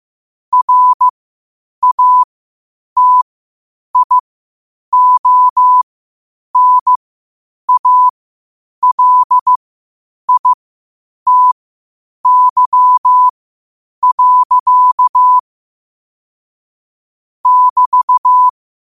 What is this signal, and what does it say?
Morse code, heard as a keyed tone, about 15 wpm, spaced as for 8 wpm, 1000 Hz. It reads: RATIONALITY. <BT>
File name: qotdfiles / 8wpm